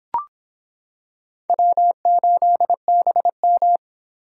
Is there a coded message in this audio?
Yes: Morse code from 1.5 to 3.8 s, heard as a keyed tone, about 25 wpm, 700 Hz. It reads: W8BM